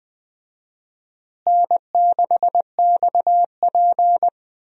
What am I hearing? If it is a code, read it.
Morse code, heard as a keyed tone, about 20 wpm, 700 Hz. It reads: N6XP
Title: USA Call Signs - 4 Characters - Repeated 2x - 20wpm